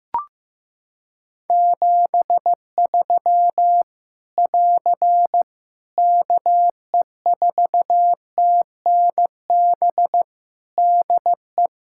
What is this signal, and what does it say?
Morse code, heard as a keyed tone, about 15 wpm, 700 Hz. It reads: 73 <AR> KE4TNB DE
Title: K1USN SST QSO Practice 15wpm